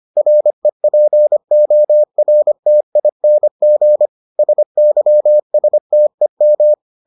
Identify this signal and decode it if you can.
Morse code, heard as a keyed tone, about 25 wpm, 600 Hz. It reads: REPORTING SYSTEM